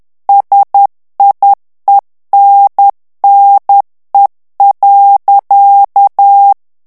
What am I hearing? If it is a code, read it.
Morse code, heard as a keyed tone, about 10 wpm, 800 Hz. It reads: SIENNE.
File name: morse mp3 / FRANCAIS TEXTE CLAIR 2 VITESSE 10 MOTS MINUTES